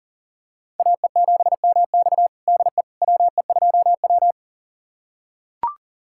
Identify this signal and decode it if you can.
Morse code, heard as a keyed tone, about 40 wpm, 700 Hz. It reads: AE7MX DE WE2W
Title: USA Callsign DE Callsign 40wpm